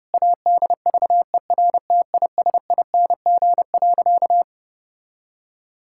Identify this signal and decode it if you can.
Morse code, heard as a keyed tone, about 32 wpm, 700 Hz. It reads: ADVERTISING.
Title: Sentences from Top 1000 Words - Review Part 6 - 30wpm